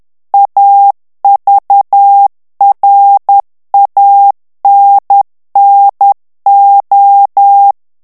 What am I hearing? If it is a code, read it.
Morse code, heard as a keyed tone, about 10 wpm, 800 Hz. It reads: AVRANNO
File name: ITALIEN texte clair 2 vitesse 10 mots minute